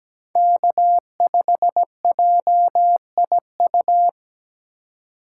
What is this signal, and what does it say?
Morse code, heard as a keyed tone, about 20 wpm, 700 Hz. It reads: K5JIU
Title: Speed Racing - USA 1x3 Call Signs 17wpm